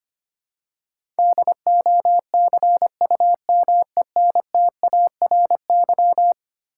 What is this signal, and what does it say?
Morse code, heard as a keyed tone, about 25 wpm, 700 Hz. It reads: DOCUMENTARY
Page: morse-code-ninja-practice-25wpm